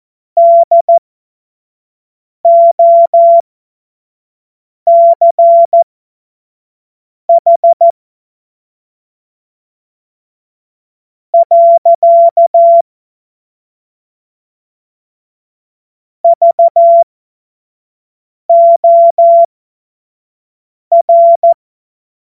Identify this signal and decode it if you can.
Morse code, heard as a keyed tone, about 14 wpm, spaced as for 4 wpm, 680 Hz. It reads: DOCH. VOR
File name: Morsekurs / Witze / Witz 09. Februar 2021, 25 F70